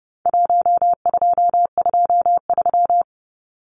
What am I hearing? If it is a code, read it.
Morse code, heard as a keyed tone, about 30 wpm, 700 Hz. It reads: 1223
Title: Numbers - Contest Style - 4-Digit 30wpm